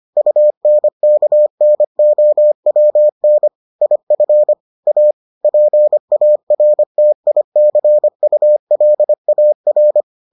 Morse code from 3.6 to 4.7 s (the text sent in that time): IF